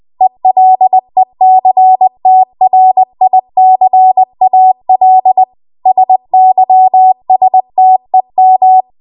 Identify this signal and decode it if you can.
Morse code, heard as a keyed tone, about 20 wpm, 750 Hz. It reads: ELECTRICAL SYSTEM
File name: Morse / 20WPM